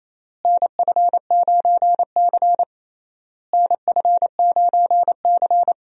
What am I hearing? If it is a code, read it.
Morse code, heard as a keyed tone, about 28 wpm, 700 Hz. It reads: NF9C NF9C
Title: USA Call Signs - 4 Characters - Repeated 3x - 28wpm